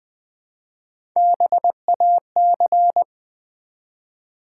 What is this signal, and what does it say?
Morse code, heard as a keyed tone, about 20 wpm, 700 Hz. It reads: BAC
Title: Build A Word - 1 Letter At A Time - 4-Letter Words - 20wpm